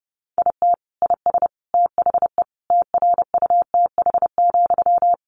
Morse code, heard as a keyed tone, about 30 wpm, 700 Hz. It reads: IT IS THE TRUTH,
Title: Sentences from Top 800 Words 30wpm